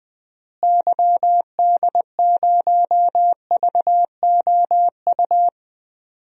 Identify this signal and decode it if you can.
Morse code, heard as a keyed tone, about 20 wpm, 700 Hz. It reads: YD0VOU
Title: DXCC-100 Call Signs - 6 Characters - Repeated 3x - 20wpm